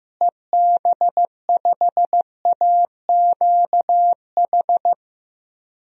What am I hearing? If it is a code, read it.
Morse code, heard as a keyed tone, about 15 wpm, 700 Hz. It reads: EB5AQH
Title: DXCC-100 Call Signs - 6 Characters - Repeated 3x - 15wpm